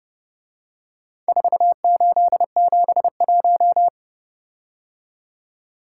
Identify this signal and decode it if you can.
Morse code, heard as a keyed tone, about 29 wpm, 700 Hz. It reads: 4871